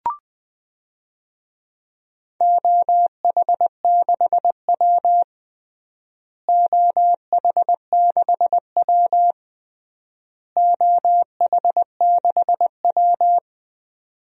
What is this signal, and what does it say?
Morse code, heard as a keyed tone, about 20 wpm, 700 Hz. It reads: OH6W OH6W OH6W
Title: DXCC-100 Call Signs - 4 Characters - Repeated 3x - 20wpm